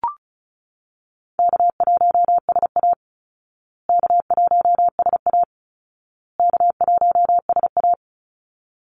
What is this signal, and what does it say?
Morse code, heard as a keyed tone, about 34 wpm, 700 Hz. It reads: K1SA K1SA K1SA
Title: Easy Call Signs Repeated 3x - 35wpm